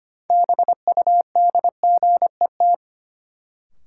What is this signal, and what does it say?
Morse code, heard as a keyed tone, about 25 wpm, 700 Hz. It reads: BUDGET
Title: Sentences from Top 1200 Words - New Word Drill - 25wpm